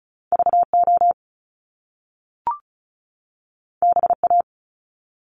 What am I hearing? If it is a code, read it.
Morse code, heard as a keyed tone, about 36 wpm, 700 Hz. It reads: VO BA
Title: Top Two Letter Combinations in Words 35wpm